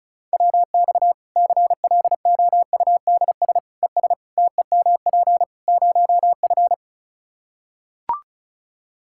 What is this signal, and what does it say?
Morse code, heard as a keyed tone, about 36 wpm, 700 Hz. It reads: WX CLOUDS ES TEMP 0F
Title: Practice QSO - Exchange 2 - Info - 35wpm